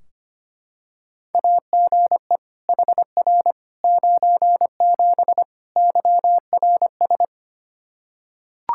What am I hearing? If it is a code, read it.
Morse code, heard as a keyed tone, about 25 wpm, 700 Hz. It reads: AGE HR 97 YRS
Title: Practice QSO - Exchange 3 - More Info - 25wpm